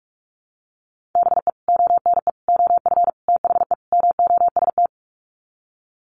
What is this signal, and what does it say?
Morse code, heard as a keyed tone, about 44 wpm, 700 Hz. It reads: BE ONE OF THE MOST
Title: Sets of 5 Words Encore 45wpm